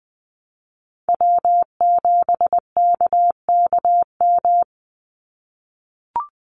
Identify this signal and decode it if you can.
Morse code, heard as a keyed tone, about 20 wpm, 700 Hz. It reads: W7KKM
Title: NA SOTA Chaser Callsigns 20wpm